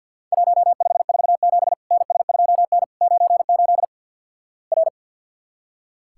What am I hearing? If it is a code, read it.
Morse code, heard as a keyed tone, about 46 wpm, 700 Hz. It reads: 1547 NS2N 98 R
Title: Worked All Europe DX Contest - QTC Exchange with Repeats - 50wpm